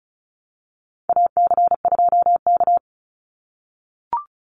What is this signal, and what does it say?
Morse code, heard as a keyed tone, about 34 wpm, 700 Hz. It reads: AC2K
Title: Call Signs - 35wpm